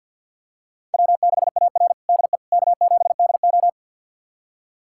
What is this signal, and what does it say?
Morse code, heard as a keyed tone, about 46 wpm, 700 Hz. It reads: W6AR DE K7DO
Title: USA Callsign DE Callsign - 4 Characters - 50wpm